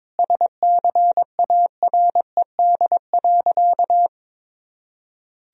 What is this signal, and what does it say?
Morse code, heard as a keyed tone, about 22 wpm, 700 Hz. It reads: SCARED.
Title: Sentences from Top 1000 Words - Review Part 4 - 22wpm